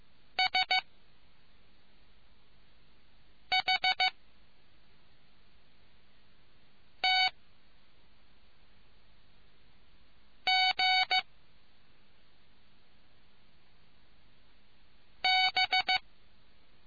Morse code, heard as a keyed tone, about 15 wpm, 2250 Hz. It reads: S H T G B